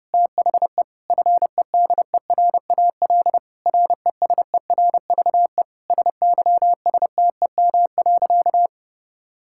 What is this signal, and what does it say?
Morse code, heard as a keyed tone, about 32 wpm, 700 Hz. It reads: THE FEDERAL RESERVE SYSTEM.
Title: Sentences from Top 1000 Words - Review Part 6 - 30wpm